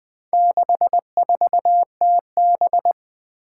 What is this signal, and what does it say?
Morse code, heard as a keyed tone, about 20 wpm, 700 Hz. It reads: 64TB
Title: Four Challenging Letter-Number Combinations 20wpm